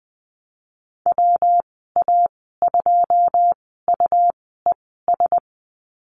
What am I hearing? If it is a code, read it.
Morse code, heard as a keyed tone, about 20 wpm, spaced as for 15 wpm, 700 Hz. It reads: WA2UES
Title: 154 Lesson 25g - Letter p - Call Signs 20wpm